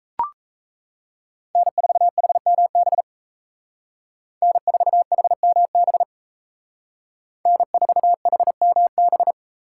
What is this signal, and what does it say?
Morse code, heard as a keyed tone, about 33 wpm, 700 Hz. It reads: N4HMB N4HMB N4HMB
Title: Speed Racing - USA 1x3 Call Signs 28wpm